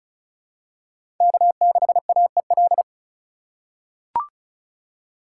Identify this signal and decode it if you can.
Morse code, heard as a keyed tone, about 34 wpm, 700 Hz. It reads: K6AEL
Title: USA Call Signs - 3x Character Spacing - 35wpm